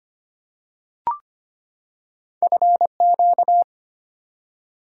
Morse code, heard as a keyed tone, about 25 wpm, 700 Hz. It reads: FQ
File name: Two Letter-Number Combinations - 25wpm